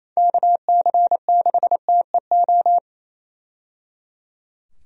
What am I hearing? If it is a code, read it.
Morse code, heard as a keyed tone, about 28 wpm, 700 Hz. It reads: KC6TEO
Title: USA Call Signs - 6 Characters - 28wpm